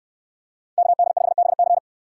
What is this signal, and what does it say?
Morse code, heard as a keyed tone, about 65 wpm, 700 Hz. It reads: KD5BC